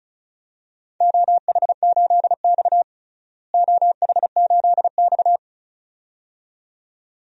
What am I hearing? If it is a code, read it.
Morse code, heard as a keyed tone, about 34 wpm, 700 Hz. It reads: OH8X OH8X
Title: DXCC-100 Call Signs - 4 Characters - Repeated 2x - 35wpm